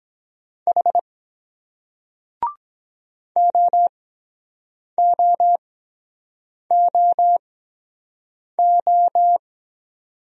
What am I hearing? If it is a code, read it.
Morse code, heard as a keyed tone, about 23 wpm, spaced as for 7 wpm, 700 Hz. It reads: H OOOO